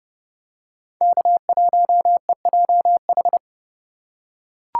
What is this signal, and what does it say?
Morse code, heard as a keyed tone, about 30 wpm, 700 Hz. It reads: K1EJH
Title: USA Call Signs - 6x Character Spacing - 30wpm